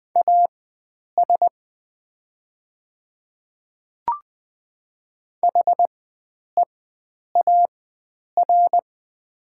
Morse code, heard as a keyed tone, about 20 wpm, spaced as for 9 wpm, 700 Hz. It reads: AS HEAR
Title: 4x Character Spacing - Top 500 Words - 20wpm